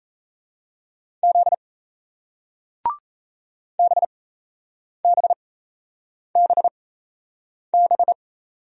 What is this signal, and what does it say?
Morse code, heard as a keyed tone, about 37 wpm, spaced as for 8 wpm, 700 Hz. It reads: Z BBBB